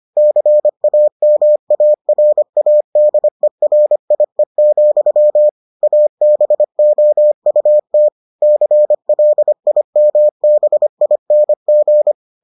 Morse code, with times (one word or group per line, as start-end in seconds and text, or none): 0.2-5.5 CAMARADERIE,
5.8-8.1 ABOUT
8.4-12.1 CLIMBING